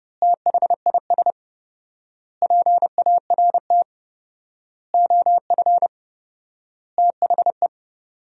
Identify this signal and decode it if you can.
Morse code, heard as a keyed tone, about 30 wpm, 700 Hz. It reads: THIS PART OF THE
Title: 4x Word-Spacing - Sentences from Top 300 Words - 30wpm